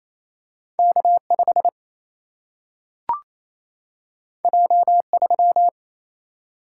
Morse code, heard as a keyed tone, about 28 wpm, 700 Hz. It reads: K5 J3
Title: Single-Letter Single-Number 28wpm